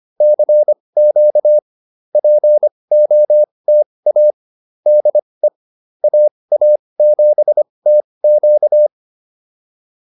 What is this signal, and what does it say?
Morse code, heard as a keyed tone, about 25 wpm, spaced as for 20 wpm, 590 Hz. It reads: CQ POTA DE AA7TQ